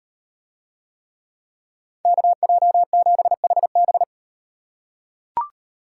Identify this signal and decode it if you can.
Morse code, heard as a keyed tone, about 36 wpm, 700 Hz. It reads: KJ7HB